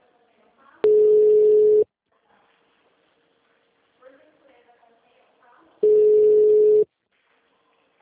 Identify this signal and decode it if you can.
Ringback tone (Europe, 425 Hz), with speech.
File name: out- -2005- - - .25206